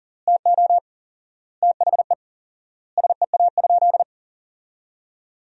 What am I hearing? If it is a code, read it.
Morse code, heard as a keyed tone, about 40 wpm, 700 Hz. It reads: TO THE SEA?
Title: 4x Word-Spacing - Sentences from Top 300 Words - 40wpm